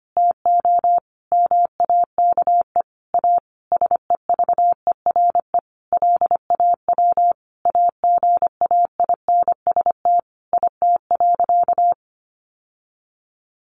Morse code, heard as a keyed tone, about 25 wpm, 700 Hz. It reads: TO MAKE A SEVERE LAW AGAINST IT.
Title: Sentences from Top 1400 Words 25wpm